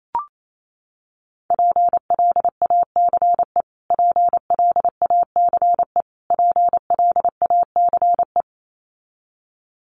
Morse code, heard as a keyed tone, about 28 wpm, 700 Hz. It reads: PLACE PLACE PLACE
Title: Top 500 Words Repeated 28wpm